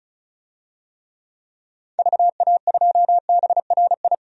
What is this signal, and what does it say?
Morse code, heard as a keyed tone, about 36 wpm, 700 Hz. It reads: VA2BRI
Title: DXCC-100 Call Signs - 6 Characters - 35wpm